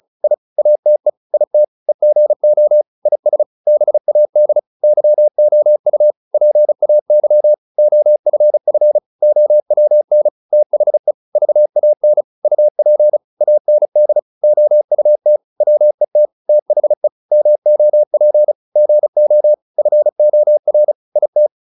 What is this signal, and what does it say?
Morse code, heard as a keyed tone, about 34 wpm, 600 Hz. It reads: I ATE IT EGO IS BAD YOU PAY OFF OWN THE VAN UP AND OUT WET THE MOP GO FOR IT